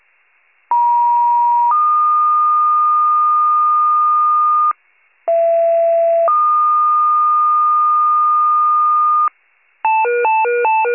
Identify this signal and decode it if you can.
Two-tone page, heard as a one-second tone, then a three-second tone, about 950 then 1220 Hz, about 670 then 1220 Hz. It is heard from 0.7 to 9.3 s.